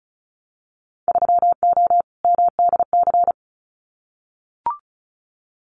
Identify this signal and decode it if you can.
Morse code, heard as a keyed tone, about 36 wpm, 700 Hz. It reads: 3O MDC
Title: Winter Field Day Exchanges 35wpm